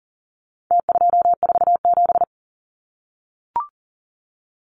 Morse code, heard as a keyed tone, about 40 wpm, 700 Hz. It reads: T247
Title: Numbers - Contest Style - 4-Digit 40wpm